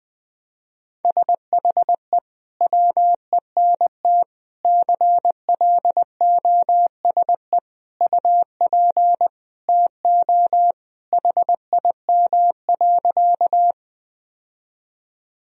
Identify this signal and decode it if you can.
Morse code, heard as a keyed tone, about 20 wpm, 700 Hz. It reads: SHE WENT CLOSE UP TO HIM.
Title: Sentences from Top 1000 Words - Review Part 1 - 20wpm